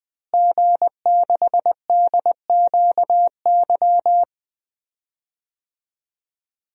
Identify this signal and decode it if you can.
Morse code, heard as a keyed tone, about 20 wpm, 700 Hz. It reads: G6DQY
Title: DXCC-100 Call Signs - 5 Characters - Repeated 2x - 20wpm